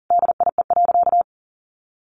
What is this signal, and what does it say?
Morse code, heard as a keyed tone, about 40 wpm, 700 Hz. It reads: DIE.
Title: Sentences from Top 1300 Words 40wpm